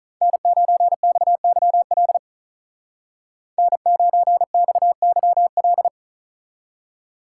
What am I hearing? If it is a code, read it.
Morse code, heard as a keyed tone, about 43 wpm, 700 Hz. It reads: N9XYL N9XYL
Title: Speed Racing - USA 1x3 Call Signs 35wpm